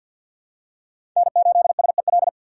Morse code, heard as a keyed tone, about 51 wpm, 700 Hz. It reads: N8SEL
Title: USA Call Signs - 5 Characters - 2x Character Spacing - 50wpm